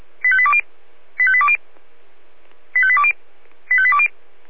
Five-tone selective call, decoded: 85820 85820 85820 85820

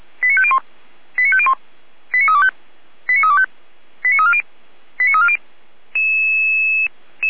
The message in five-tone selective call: 89501 89501 89325 89325 89340 89340

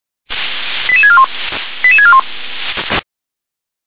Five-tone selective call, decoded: 88641 88641

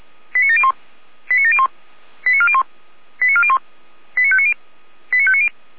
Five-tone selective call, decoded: 89701 89701 89401 89401 89590 89590

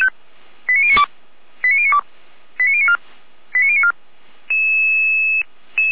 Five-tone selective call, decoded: 89092 89092 89094 89094

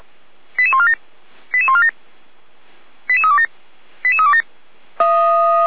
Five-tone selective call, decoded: 88147 88147 88327 88327

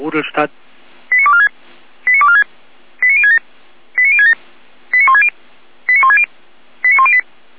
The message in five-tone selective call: 89246 89246 89067 89067 89160 89160 89108

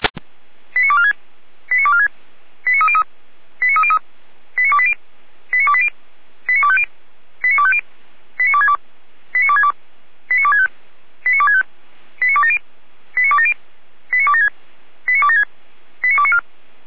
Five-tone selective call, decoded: 89246 89246 89303 89303 89280 89280 89250 89250 89262 89262 89265 89265 89270 89270 89276 89276 89294

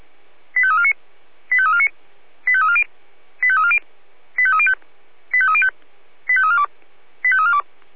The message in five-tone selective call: 85349 85349 85340 85340 85305 85305 85342 85342

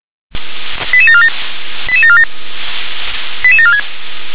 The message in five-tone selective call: 88747 88747 88746

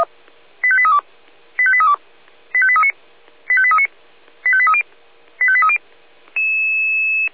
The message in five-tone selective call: 85832 85832 85839 85839 85830 85830

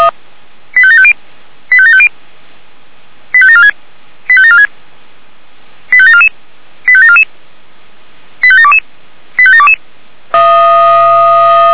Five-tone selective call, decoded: 85750 85750 85746 85746 85744 85744 85722 85722

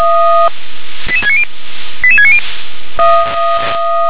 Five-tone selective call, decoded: 88690 88690

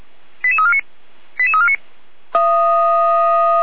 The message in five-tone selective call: 88349 88349

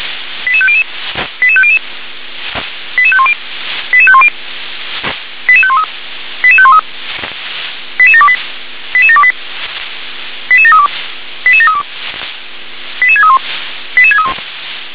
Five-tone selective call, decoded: 88500 88500 88510 88510 88513 88513 88728 88728 88732 88732 88631 88631

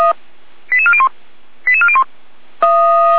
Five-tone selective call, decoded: 88401 88401

